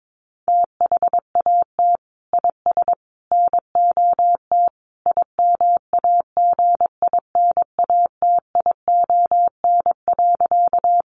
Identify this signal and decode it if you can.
Morse code, heard as a keyed tone, about 22 wpm, 700 Hz. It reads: THAT IS NOT IMAGINATION.
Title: Sentences from Top 1300 Words 22wpm